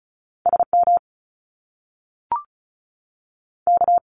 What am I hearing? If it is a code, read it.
Morse code, heard as a keyed tone, about 36 wpm, 700 Hz. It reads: S M K
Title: Top Two Letter Combinations in Words 35wpm